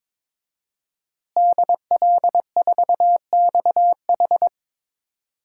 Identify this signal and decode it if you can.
Morse code, heard as a keyed tone, about 22 wpm, 700 Hz. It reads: DL4XH
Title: DXCC-100 Call Signs - 5 Characters - Repeated 3x - 22wpm